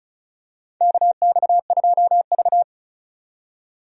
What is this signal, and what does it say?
Morse code, heard as a keyed tone, about 34 wpm, 700 Hz. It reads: KX2V
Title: USA Call Signs - 4 Characters - 35wpm